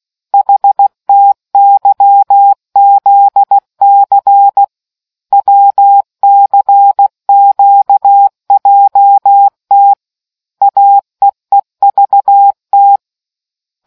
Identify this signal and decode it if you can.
Morse code, heard as a keyed tone, about 16 wpm, 800 Hz. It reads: HTYZC WCQJT AEEVT